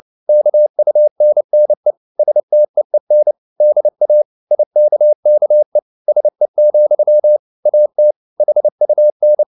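Morse code, heard as a keyed tone, about 29 wpm, 600 Hz. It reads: KUNNE STEEN DA IKKE SE, AT HUN
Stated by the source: Fast (QRQ) Morse at 29 wpm, from CW med Gnister.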